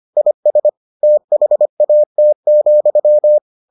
Morse code, heard as a keyed tone, about 25 wpm, 600 Hz. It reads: IS THAT,